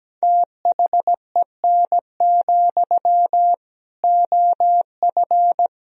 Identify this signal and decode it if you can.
Morse code, heard as a keyed tone, about 17 wpm, 700 Hz. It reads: THEN, OF
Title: Sentences from Top 1000 Words - Review Part 1 - 17wpm